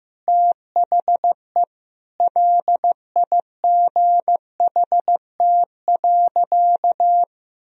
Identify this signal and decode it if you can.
Morse code, heard as a keyed tone, about 15 wpm, 700 Hz. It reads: THE LIGHT.
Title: Sentences from Top 1000 Words - Review Part 2 - 15wpm